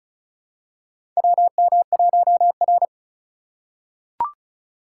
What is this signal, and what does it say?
Morse code, heard as a keyed tone, about 36 wpm, 700 Hz. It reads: WM1R
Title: USA Call Signs - 4 Characters - 4x Character Spacing - 35wpm